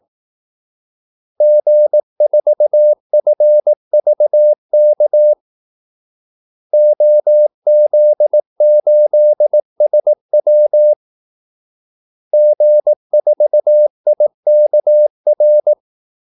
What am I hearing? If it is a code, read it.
Morse code, heard as a keyed tone, about 18 wpm, 600 Hz. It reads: G4FVK OZ8SW G4IKR
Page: Kaldesignaler 18 wpm